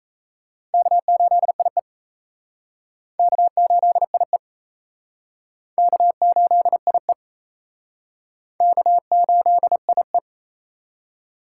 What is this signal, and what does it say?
Morse code, heard as a keyed tone, about 36 wpm, 700 Hz. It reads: K8IE K8IE K8IE K8IE